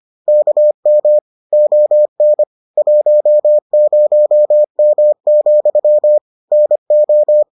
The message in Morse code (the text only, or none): KM ON 10M, NO